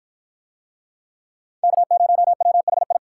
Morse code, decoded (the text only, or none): K0WHI